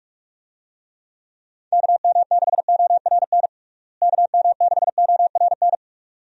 Morse code, heard as a keyed tone, about 46 wpm, 700 Hz. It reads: KM6ORN KM6ORN
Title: USA Call Signs - 6 Characters - Repeated 2x - 45wpm